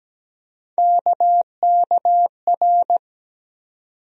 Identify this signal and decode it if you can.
Morse code, heard as a keyed tone, about 17 wpm, 700 Hz. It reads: KKR
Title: USA Call Sign Postfixes - 3 Letters 17wpm